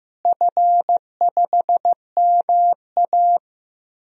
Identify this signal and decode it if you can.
Morse code, heard as a keyed tone, about 15 wpm, 700 Hz. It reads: F5MA
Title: DXCC-100 Call Signs - 4 Characters - Repeated 3x - 15wpm